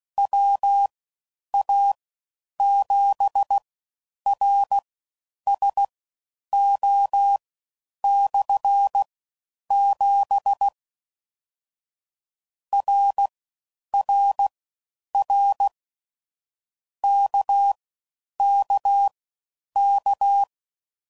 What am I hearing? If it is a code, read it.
Morse code, heard as a keyed tone, about 16 wpm, spaced as for 9 wpm, 800 Hz. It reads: WA7RSO/7 RRR KKK